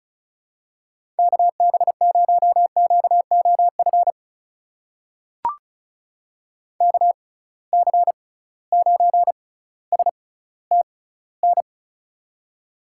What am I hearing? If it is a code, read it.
Morse code, heard as a keyed tone, about 34 wpm, spaced as for 18 wpm, 700 Hz. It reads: KB0QOF KC9STN
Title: USA Call Signs - 6x Character Spacing - 35wpm